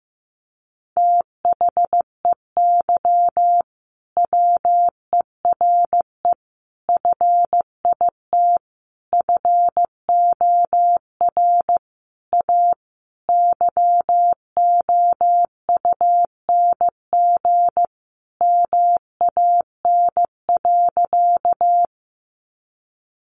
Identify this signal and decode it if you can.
Morse code, heard as a keyed tone, about 15 wpm, 700 Hz. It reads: THEY WERE FIT FOR A YOUNG MAN.
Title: Sentences from Top 800 Words 15wpm